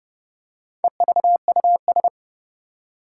Morse code, heard as a keyed tone, about 32 wpm, 700 Hz. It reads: EVUS
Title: Four Challenging Letter-Number Combinations 30wpm